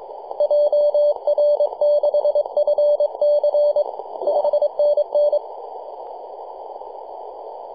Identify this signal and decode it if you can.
Morse code, heard as a keyed tone, about 23 wpm, 590 Hz. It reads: JR6FC 5NN
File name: 3d2ya-18-085-cw